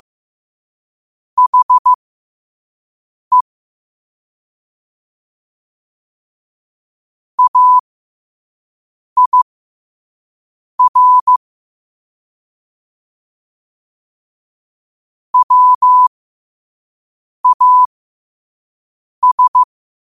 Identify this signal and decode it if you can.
Morse code, heard as a keyed tone, about 15 wpm, spaced as for 5 wpm, 1000 Hz. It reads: HE AIR WAS
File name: qotdfiles / 5wpm